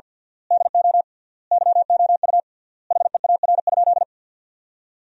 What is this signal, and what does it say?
Morse code, heard as a keyed tone, about 52 wpm, 700 Hz. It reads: DO YOU HEAR?
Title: 3x Word-Spacing - Sentences from Top 300 Words - 50wpm